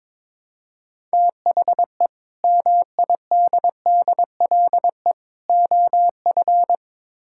Morse code, heard as a keyed tone, about 22 wpm, 700 Hz. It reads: THE MIDDLE OF
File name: practice / 22wpm / Build A Phrase - 5-Word Phrases - 22wpm